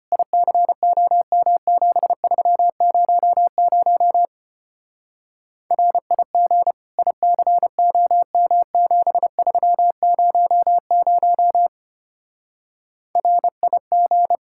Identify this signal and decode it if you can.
Morse code, heard as a keyed tone, about 32 wpm, 700 Hz. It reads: ICOM7300 RIG ICOM7300 RIG